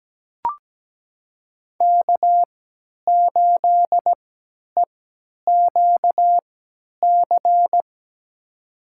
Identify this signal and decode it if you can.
Morse code, heard as a keyed tone, about 17 wpm, spaced as for 10 wpm, 700 Hz. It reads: K8EQC
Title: USA Call Signs - 5 Characters - 3x Character Spacing - 17wpm